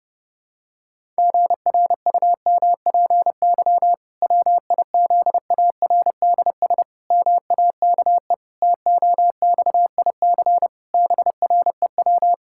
Morse code, heard as a keyed tone, about 30 wpm, 700 Hz. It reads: GRUMPY WIZARDS MAKE TOXIC BREW
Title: Pangram Challenge v2 - 30wpm